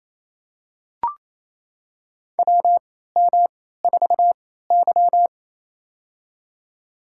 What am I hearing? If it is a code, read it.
Morse code, heard as a keyed tone, about 27 wpm, spaced as for 16 wpm, 700 Hz. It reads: WM4Y
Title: USA Call Signs - 3x Character Spacing - 28wpm